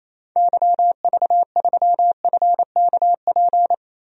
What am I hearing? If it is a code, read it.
Morse code, heard as a keyed tone, about 28 wpm, 700 Hz. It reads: YV3FKP
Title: DXCC-100 Call Signs - 6 Characters - Repeated 2x - 28wpm